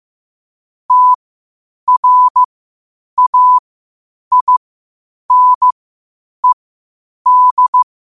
Morse code, heard as a keyed tone, about 15 wpm, spaced as for 8 wpm, 1000 Hz. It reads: TRAINED
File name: qotdfiles / 8wpm